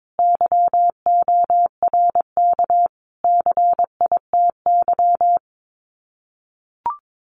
Morse code, heard as a keyed tone, about 22 wpm, 700 Hz. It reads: YORK CITY
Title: Sets of 5 Words Encore2 22wpm